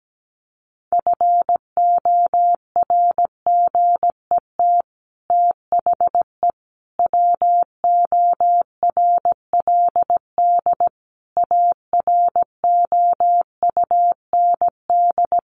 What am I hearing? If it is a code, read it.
Morse code, heard as a keyed tone, about 17 wpm, 700 Hz. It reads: FORGET THE WORLD AROUND
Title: Sentences from Top 1200 Words 17wpm